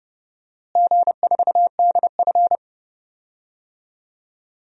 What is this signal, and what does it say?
Morse code, heard as a keyed tone, about 32 wpm, 700 Hz. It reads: G4DF
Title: DXCC-100 Call Signs - 4 Characters - Repeated 2x - 30wpm